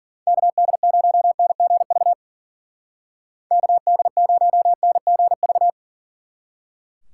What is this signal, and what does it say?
Morse code, heard as a keyed tone, about 50 wpm, 700 Hz. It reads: KD0NGV KD0NGV
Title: Speed Racing - USA 2x3 Call Signs 40wpm